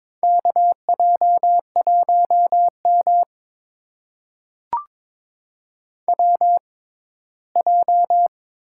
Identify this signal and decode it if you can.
Morse code, heard as a keyed tone, about 22 wpm, spaced as for 15 wpm, 700 Hz. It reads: KJ1M WJ
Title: USA Call Signs - 4 Characters - 6x Character Spacing - 22wpm